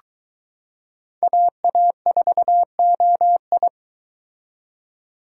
Morse code, heard as a keyed tone, about 23 wpm, 700 Hz. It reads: AA4OI